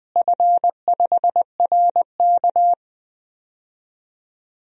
Morse code, heard as a keyed tone, about 20 wpm, 700 Hz. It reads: F5RK
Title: DXCC-100 Call Signs - 4 Characters - Repeated 3x - 20wpm